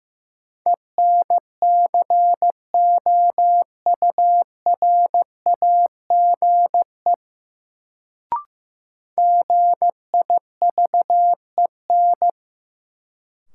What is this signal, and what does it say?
Morse code, heard as a keyed tone, about 15 wpm, 700 Hz. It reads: ENCOURAGE GIVEN
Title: Sentences from Top 1400 Words - New Word Drill Alt - 15wpm